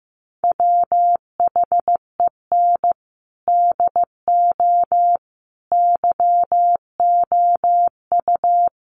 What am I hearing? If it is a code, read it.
Morse code, heard as a keyed tone, about 14 wpm, 700 Hz. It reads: WHEN DO YOU
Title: Sentences from Top 1100 Words 15wpm